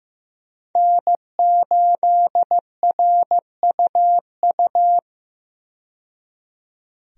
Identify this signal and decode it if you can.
Morse code, heard as a keyed tone, about 15 wpm, 700 Hz. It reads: N8RUU
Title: USA Call Signs - 5 Characters - 15wpm